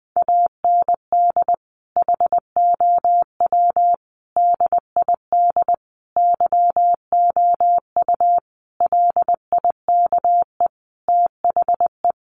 Morse code, heard as a keyed tone, about 20 wpm, 700 Hz. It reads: AND HOW DID YOU LIKE THE
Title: Sentences from Top 1200 Words 20wpm